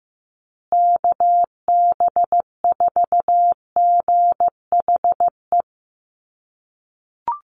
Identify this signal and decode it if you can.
Morse code, heard as a keyed tone, about 15 wpm, 700 Hz. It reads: KB4GHE
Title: USA Call Signs Repeated 3x 15wpm